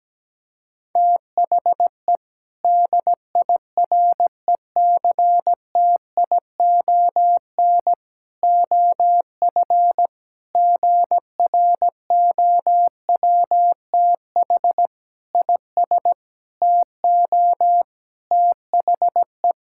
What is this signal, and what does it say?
Morse code, heard as a keyed tone, about 17 wpm, 700 Hz. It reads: THE DIRECTION OF GROWTH IS TO THE
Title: Sentences from Top 1000 Words - Review Part 5 - 17wpm